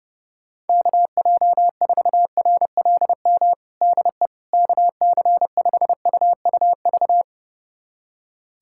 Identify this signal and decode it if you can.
Morse code, heard as a keyed tone, about 30 wpm, 700 Hz. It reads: KJ4RLM DE KC5UUV
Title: USA Callsign DE Callsign - 6 Characters - 30wpm